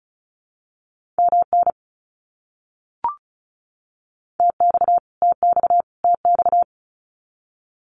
Morse code, heard as a keyed tone, about 34 wpm, 700 Hz. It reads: MN TX TX TX
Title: US States Repeated 3x 35wpm